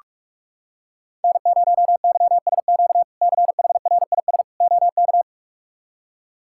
Morse code, heard as a keyed tone, about 46 wpm, 700 Hz. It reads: N0YSQ CHRIS OK
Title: K1USN SST QSO Practice - 45wpm